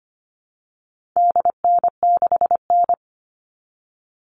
Morse code, heard as a keyed tone, about 25 wpm, 700 Hz. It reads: DN6N
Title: All Cut Numbers - 4-Digit 25wpm